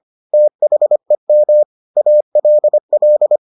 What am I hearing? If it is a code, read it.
Morse code, heard as a keyed tone, about 25 wpm, 600 Hz. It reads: THEM ALL